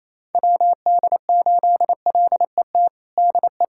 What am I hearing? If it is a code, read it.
Morse code, heard as a keyed tone, about 28 wpm, 700 Hz. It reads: WD8LET DE
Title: USA Callsign DE Callsign 28wpm